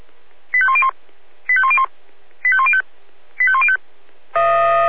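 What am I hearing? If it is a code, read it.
Five-tone selective call (ZVEI), heard as quick runs of five tones, tones of about 70 ms. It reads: 85101 85101 85105 85105
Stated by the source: Five-tone ZVEI selective call received by SDR.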